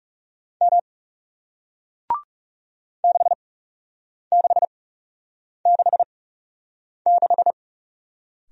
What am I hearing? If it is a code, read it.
Morse code, heard as a keyed tone, about 40 wpm, spaced as for 8 wpm, 700 Hz. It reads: M 6666